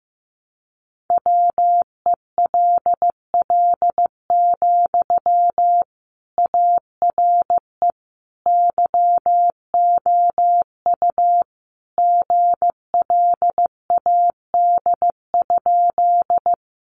Morse code, heard as a keyed tone, about 15 wpm, 700 Hz. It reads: WELL, ARE YOU GLAD?
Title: Sentences from Top 800 Words 15wpm